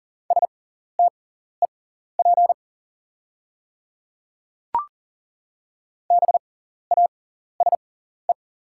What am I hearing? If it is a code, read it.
Morse code, heard as a keyed tone, about 40 wpm, spaced as for 14 wpm, 700 Hz. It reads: STEP BASE
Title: 6x Character Spacing - Top 500 Words - 40wpm